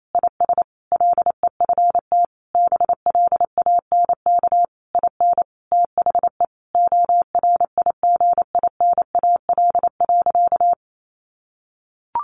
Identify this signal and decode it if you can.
Morse code, heard as a keyed tone, about 28 wpm, 700 Hz. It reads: IS LEFT BLANK IN THE ORIGINAL.
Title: Sentences from Top 1500 Words 28wpm